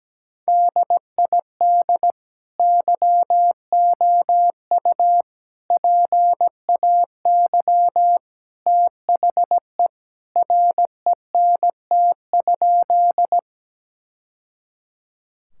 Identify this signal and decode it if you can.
Morse code, heard as a keyed tone, about 17 wpm, 700 Hz. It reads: DID YOU PAY THE RENT?
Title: Sentences from Top 1000 Words - Review Part 5 - 17wpm